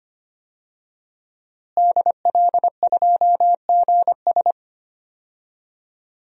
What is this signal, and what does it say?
Morse code, heard as a keyed tone, about 25 wpm, 700 Hz. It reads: DL2GS
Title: DXCC-100 Call Signs - 5 Characters - Repeated 3x - 25wpm